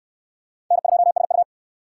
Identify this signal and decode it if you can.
Morse code, heard as a keyed tone, about 68 wpm, 700 Hz. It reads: N2SU